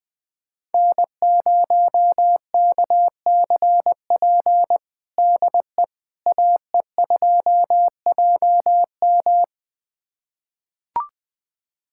Morse code, heard as a keyed tone, about 20 wpm, 700 Hz. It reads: N0KCP DE AE2JM
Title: USA Callsign DE Callsign 20wpm